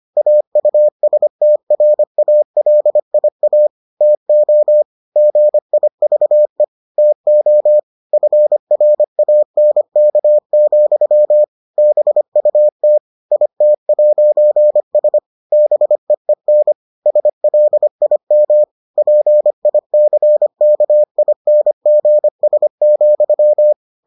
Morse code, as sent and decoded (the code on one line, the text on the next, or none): .- ..- ... - .-. .- .-.. .. .- / - --- / --. .. ...- . / - --- / ..-. .-. .- -. -.- --..-- / -... ..- - / .. - .----. ... / -... . -. / ... .-.. .. -- / .--. .. -.-. -.- .. -. --. ... --..--
AUSTRALIA TO GIVE TO FRANK, BUT IT'S BEEN SLIM PICKINGS,